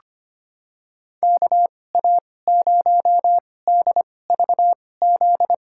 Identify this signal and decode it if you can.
Morse code, heard as a keyed tone, about 25 wpm, spaced as for 18 wpm, 700 Hz. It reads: KA0DVZ
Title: USA Call Signs - 6 Characters - 2x Character Spacing - 25wpm